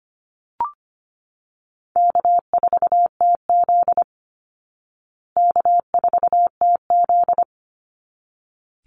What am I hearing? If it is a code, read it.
Morse code, heard as a keyed tone, about 25 wpm, 700 Hz. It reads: K4TZ K4TZ
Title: Call Signs Repeated 3x - 25wpm